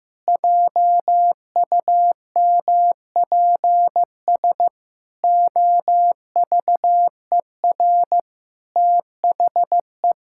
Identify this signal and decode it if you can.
Morse code, heard as a keyed tone, about 15 wpm, 700 Hz. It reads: JUMPS OVER THE
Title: Pangram Challenge v2 - 15wpm